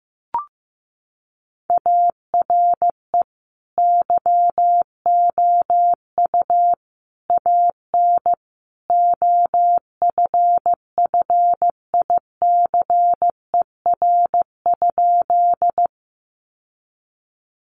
Morse code, heard as a keyed tone, about 14 wpm, 700 Hz. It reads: ARE YOU AN OFFICER?
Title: Sentences from Top 1200 Words 15wpm